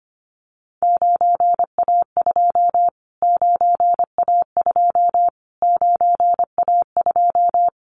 Morse code, heard as a keyed tone, about 25 wpm, 700 Hz. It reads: 9A2 9A2 9A2
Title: Callsign Prefixes for Top 25 DXCC - Repeated 3x - 25wpm